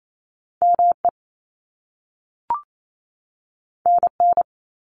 Morse code, heard as a keyed tone, about 28 wpm, 700 Hz. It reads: ME NN
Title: Top Two Letter Combinations in Words 28wpm